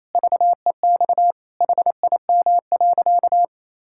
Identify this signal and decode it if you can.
Morse code, heard as a keyed tone, about 28 wpm, 700 Hz. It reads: VEX HIM.
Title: Pangram Challenge v2 - 28wpm